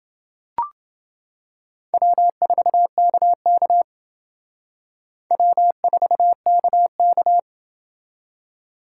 Morse code, heard as a keyed tone, about 29 wpm, 700 Hz. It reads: W4KK W4KK